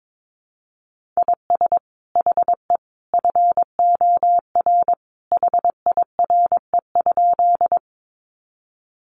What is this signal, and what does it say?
Morse code, heard as a keyed tone, about 22 wpm, 700 Hz. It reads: IS HE FOR HIRE?
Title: Sentences from Top 1400 Words 22wpm